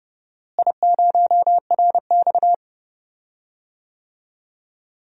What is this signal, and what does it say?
Morse code, heard as a keyed tone, about 32 wpm, 700 Hz. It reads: I0RX
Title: DXCC-100 Call Signs - 4 Characters - Repeated 2x - 30wpm